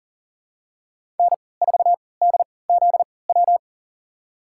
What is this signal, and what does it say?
Morse code, heard as a keyed tone, about 40 wpm, spaced as for 23 wpm, 700 Hz. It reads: N4DZW
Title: USA Call Signs - 3x Character Spacing - 40wpm